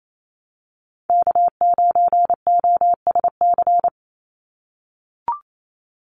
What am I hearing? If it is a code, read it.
Morse code, heard as a keyed tone, about 28 wpm, 700 Hz. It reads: K9OSC
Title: NA SOTA Chaser Callsigns 28wpm